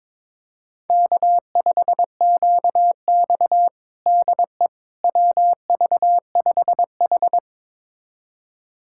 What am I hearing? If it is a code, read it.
Morse code, heard as a keyed tone, about 21 wpm, 700 Hz. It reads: K5QX DE WV5H
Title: USA Callsign DE Callsign - 4 Characters - 22wpm